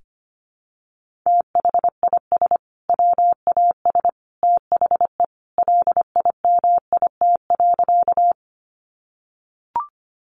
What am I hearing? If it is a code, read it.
Morse code, heard as a keyed tone, about 25 wpm, 700 Hz. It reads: THIS WAS THE LIMIT.
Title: Sentences from Top 1500 Words 25wpm